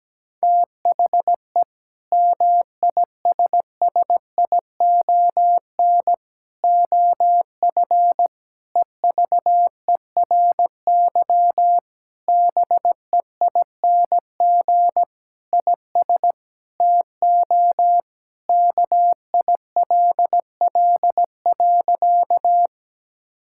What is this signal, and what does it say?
Morse code, heard as a keyed tone, about 17 wpm, 700 Hz. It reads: THE MISSION OF EVERY BEING IS TO KILL.
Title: Sentences from Top 1600 Words 17wpm